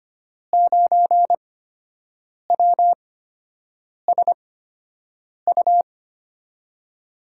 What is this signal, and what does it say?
Morse code, heard as a keyed tone, about 25 wpm, spaced as for 7 wpm, 700 Hz. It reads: 9WSU
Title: USA Call Signs - 5 Characters - 8x Character Spacing - 25wpm